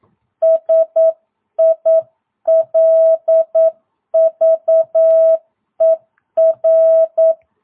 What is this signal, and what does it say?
Morse code, heard as a keyed tone, about 9 wpm, 660 Hz. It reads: SILVER
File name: jh8cbh / 2022cw